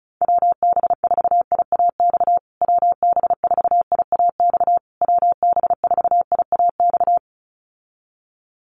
Morse code, heard as a keyed tone, about 34 wpm, 700 Hz. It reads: WB4IAX WB4IAX WB4IAX
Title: USA Call Signs Repeated 3x 35wpm